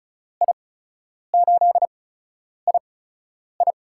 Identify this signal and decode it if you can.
Morse code, heard as a keyed tone, about 36 wpm, spaced as for 10 wpm, 700 Hz. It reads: I8II
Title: USA Call Signs - 5 Characters - 8x Character Spacing - 35wpm